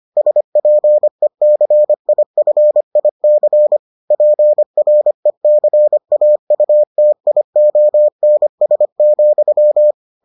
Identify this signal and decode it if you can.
Morse code, heard as a keyed tone, about 25 wpm, 600 Hz. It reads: SPECIFIC PRECAUTIONS,